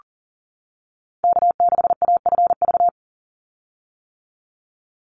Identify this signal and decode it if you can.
Morse code, heard as a keyed tone, about 40 wpm, 700 Hz. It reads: K6AFV
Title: USA Call Signs 40wpm